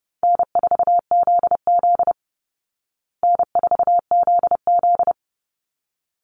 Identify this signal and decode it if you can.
Morse code, heard as a keyed tone, about 30 wpm, 700 Hz. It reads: N4ZZ N4ZZ
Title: Easy Call Signs Repeated 2x - 30wpm